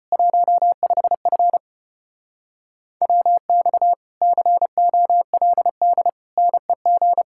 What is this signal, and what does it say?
Morse code, heard as a keyed tone, about 29 wpm, 700 Hz. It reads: 15F WX COLD NEG
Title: Speed-Racing Common QSO Phrases - 25wpm